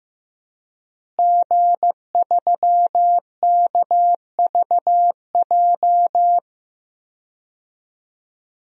Morse code, heard as a keyed tone, about 15 wpm, 700 Hz. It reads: G3KVJ
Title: DXCC-100 Call Signs - 5 Characters - Repeated 2x - 15wpm